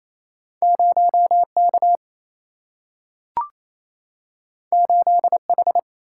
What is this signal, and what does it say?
Morse code, heard as a keyed tone, about 28 wpm, 700 Hz. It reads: 0 K 8 H
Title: Single-Letter Single-Number 28wpm